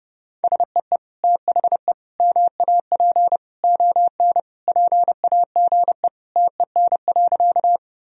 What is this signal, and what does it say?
Morse code, heard as a keyed tone, about 32 wpm, 700 Hz. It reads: SEE THE MAP ON PAGE TEN.
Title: Sentences from Top 1000 Words - Review Part 2 - 30wpm